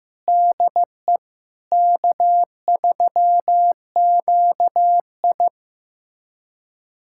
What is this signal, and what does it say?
Morse code, heard as a keyed tone, about 15 wpm, 700 Hz. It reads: DE K3QI
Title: USA Callsign DE Callsign - 4 Characters - 15wpm